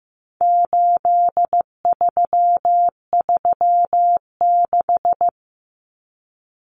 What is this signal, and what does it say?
Morse code, heard as a keyed tone, about 15 wpm, 700 Hz. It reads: 8336
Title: Numbers - 4-Digit 15wpm